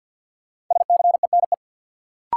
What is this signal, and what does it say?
Morse code, heard as a keyed tone, about 50 wpm, 700 Hz. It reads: SCENE
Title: Sentences from Top 1100 Words - New Word Drill - 50wpm